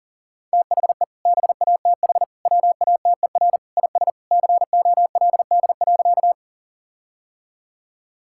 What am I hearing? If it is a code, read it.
Morse code, heard as a keyed tone, about 40 wpm, 700 Hz. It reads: THE BATH WATER IS COLD.
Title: Sentences from Top 1600 Words 40wpm